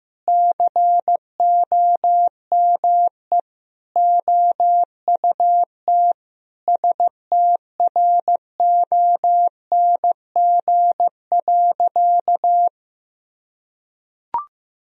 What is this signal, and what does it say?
Morse code, heard as a keyed tone, about 15 wpm, 700 Hz. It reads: COME OUT STRONG.
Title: Sentences from Top 1000 Words - Review Part 3 - 15wpm